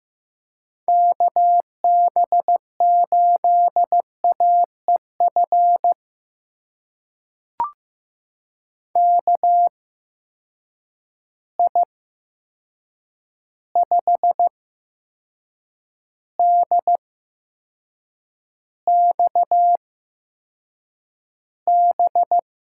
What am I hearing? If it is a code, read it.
Morse code, heard as a keyed tone, about 15 wpm, spaced as for 6 wpm, 700 Hz. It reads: KB8AEF KI5DXB
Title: USA Call Signs - 6 Characters - 8x Character Spacing - 15wpm